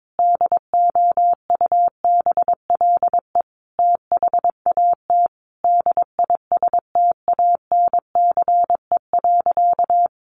Morse code, heard as a keyed tone, about 22 wpm, 700 Hz. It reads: DOUBLE THAT DISTANCE.
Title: Sentences from Top 1400 Words 22wpm